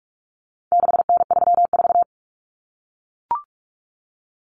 Morse code, heard as a keyed tone, about 44 wpm, 700 Hz. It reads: 6N34
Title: Common Cut Numbers - 4-Digit 45wpm